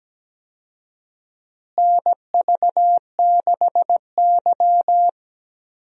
Morse code, heard as a keyed tone, about 17 wpm, 700 Hz. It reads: NV6Y
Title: USA Call Signs - 4 Characters - Repeated 3x - 17wpm